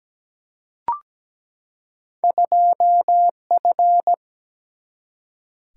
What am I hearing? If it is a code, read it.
Morse code, heard as a keyed tone, about 17 wpm, 700 Hz. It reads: 2F
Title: Single-Letter Single-Number 17wpm